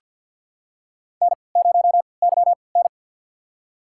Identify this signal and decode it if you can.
Morse code, heard as a keyed tone, about 48 wpm, spaced as for 28 wpm, 700 Hz. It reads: N0YN
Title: USA Call Signs - 4 Characters - 3x Character Spacing - 50wpm